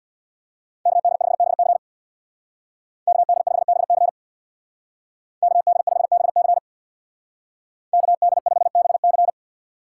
Morse code, heard as a keyed tone, about 60 wpm, 700 Hz. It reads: KD5BC KD5BC KD5BC KD5BC
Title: Speed Racing - USA 2x2 Call Signs 50wpm